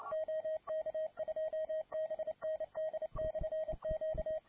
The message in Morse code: OK2BND/Q

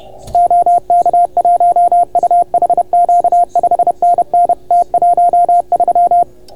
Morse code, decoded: OK1UHQ 5NN T13

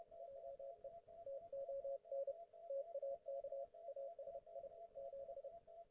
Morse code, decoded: JE TON TAK ASI 7

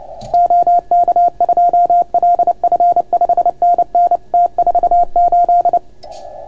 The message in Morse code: OK2LF 5NN T48

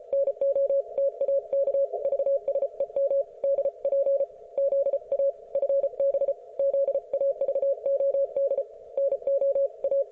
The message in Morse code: NOTAK VSEM DP ZA FB ZAVOD NOA

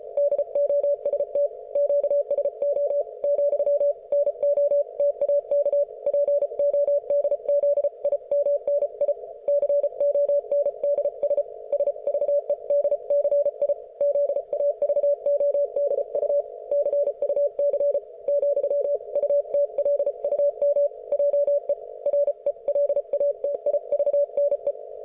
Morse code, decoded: DOST QSO , NO TAK PODZIMNI CONDS SVEDCI ZAVODU CUC , UTLUM JE RELATIVNE

